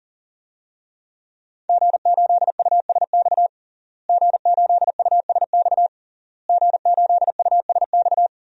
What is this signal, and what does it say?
Morse code, heard as a keyed tone, about 40 wpm, 700 Hz. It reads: G8USX G8USX G8USX
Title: DXCC-100 Call Signs - 5 Characters - Repeated 3x - 40wpm